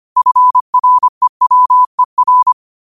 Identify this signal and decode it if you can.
Morse code, heard as a keyed tone, about 25 wpm, 1000 Hz. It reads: FREWER
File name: qotdfiles / 25wpm